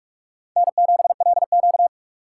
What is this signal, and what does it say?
Morse code, heard as a keyed tone, about 42 wpm, 690 Hz. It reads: N7LQ
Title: USA Call Signs - 4 Characters - 3x Character Spacing - 45wpm